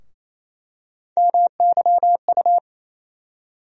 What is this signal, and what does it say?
Morse code, heard as a keyed tone, about 27 wpm, 700 Hz. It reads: MYU
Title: USA Call Sign Postfixes - 3 Letters 28wpm